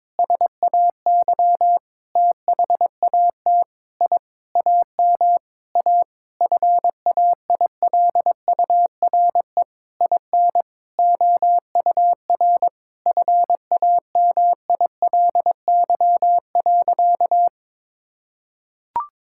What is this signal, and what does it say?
Morse code, heard as a keyed tone, about 22 wpm, 700 Hz. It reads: SAY THAT I AM A FAILURE IN OUR FAMILY.
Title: Sentences from Top 1000 Words - Review Part 6 - 22wpm